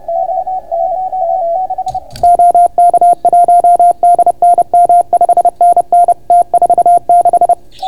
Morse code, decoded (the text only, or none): D N M OK1DNM 5NN T46